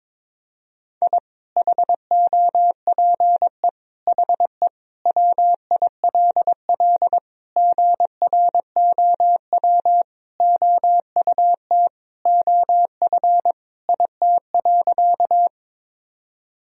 Morse code, heard as a keyed tone, about 22 wpm, 700 Hz. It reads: I HOPE HE WILL GROW OUT OF IT.
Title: Sentences from Top 1000 Words - Review Part 1 - 22wpm